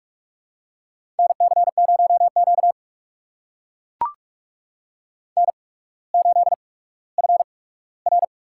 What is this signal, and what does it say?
Morse code, heard as a keyed tone, about 44 wpm, spaced as for 19 wpm, 700 Hz. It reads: NC0Q N8FR
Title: USA Call Signs - 4 Characters - 8x Character Spacing - 45wpm